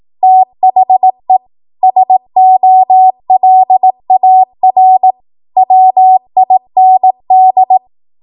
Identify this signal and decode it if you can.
Morse code, heard as a keyed tone, about 18 wpm, 750 Hz. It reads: THE SOLAR WIND
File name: Morse / 18WPM